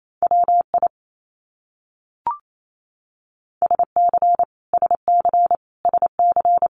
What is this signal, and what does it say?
Morse code, heard as a keyed tone, about 28 wpm, 700 Hz. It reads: WI SC SC SC
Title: US States Repeated 3x 28wpm